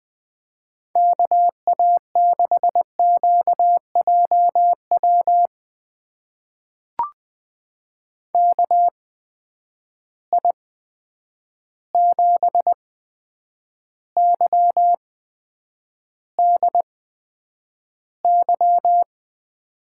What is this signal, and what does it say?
Morse code, heard as a keyed tone, about 20 wpm, spaced as for 9 wpm, 700 Hz. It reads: KA6QJW KI7YDY
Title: USA Call Signs - 6 Characters - 8x Character Spacing - 20wpm